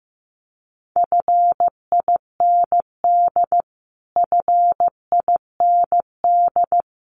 Morse code, heard as a keyed tone, about 15 wpm, 700 Hz. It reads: FIND FIND
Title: Top 300 Words Repeated 15wpm